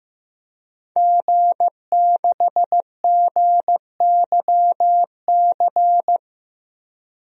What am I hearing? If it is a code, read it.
Morse code, heard as a keyed tone, about 16 wpm, 700 Hz. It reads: G6GYC
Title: DXCC-100 Call Signs - 5 Characters - Repeated 2x - 15wpm